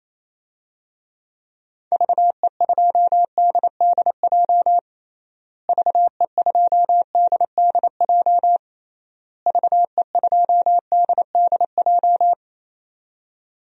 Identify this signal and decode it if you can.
Morse code, heard as a keyed tone, about 28 wpm, 700 Hz. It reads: VE2DDJ VE2DDJ VE2DDJ
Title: DXCC-100 Call Signs - 6 Characters - Repeated 3x - 28wpm